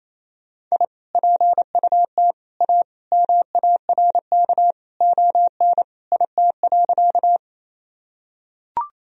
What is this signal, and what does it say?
Morse code, heard as a keyed tone, about 28 wpm, 700 Hz. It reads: I PUT A MARK ON IT.
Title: Sentences from Top 1000 Words - Review Part 1 - 28wpm